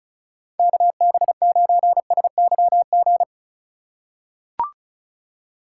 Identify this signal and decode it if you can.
Morse code, heard as a keyed tone, about 36 wpm, 700 Hz. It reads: KB9SYG
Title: USA Call Signs - 6 Characters - 3x Character Spacing - 35wpm